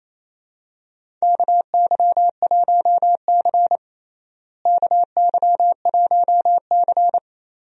Morse code, heard as a keyed tone, about 28 wpm, 700 Hz. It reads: KY1C KY1C
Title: USA Call Signs - 4 Characters - Repeated 2x - 28wpm